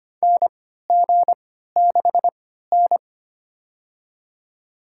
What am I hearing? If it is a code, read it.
Morse code, heard as a keyed tone, about 25 wpm, spaced as for 14 wpm, 700 Hz. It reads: NG6N
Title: USA Call Signs - 4 Characters - 3x Character Spacing - 25wpm